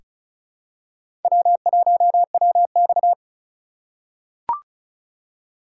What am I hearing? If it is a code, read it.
Morse code, heard as a keyed tone, about 36 wpm, 700 Hz. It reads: W1WX
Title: USA Call Signs - 4 Characters - 4x Character Spacing - 35wpm